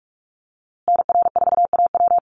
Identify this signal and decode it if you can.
Morse code, heard as a keyed tone, about 44 wpm, 690 Hz. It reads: NR4AW
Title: USA Call Signs Repeated 3x 45wpm